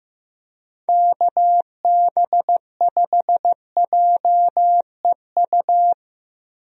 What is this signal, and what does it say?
Morse code, heard as a keyed tone, about 15 wpm, 700 Hz. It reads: KB5JEU
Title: USA Call Signs - 6 Characters - Repeated 2x - 15wpm